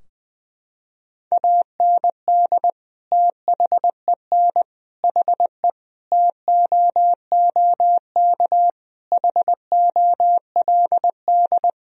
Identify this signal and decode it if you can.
Morse code, heard as a keyed tone, about 20 wpm, 700 Hz. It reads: AND THEN HE TOOK HOLD
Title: Sentences from Top 1000 Words - Review Part 2 - 20wpm